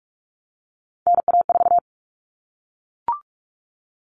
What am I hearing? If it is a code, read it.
Morse code, heard as a keyed tone, about 44 wpm, 700 Hz. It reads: NA4
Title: All Cut Numbers - 3-Digit 45wpm